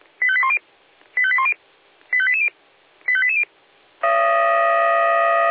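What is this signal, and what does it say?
Five-tone selective call (ZVEI), heard as quick runs of five tones, tones of about 80 ms. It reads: 85810 85810 85880 85880
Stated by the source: ZVEI five-tone selective call received on SDR.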